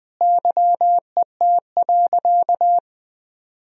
Morse code, heard as a keyed tone, about 20 wpm, 700 Hz. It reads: YET.
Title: Sentences from Top 1000 Words - Review Part 2 - 20wpm